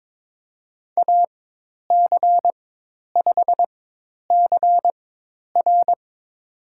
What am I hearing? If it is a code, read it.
Morse code, heard as a keyed tone, about 22 wpm, spaced as for 10 wpm, 700 Hz. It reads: AC5CR
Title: USA Call Signs - 5 Characters - 4x Character Spacing - 22wpm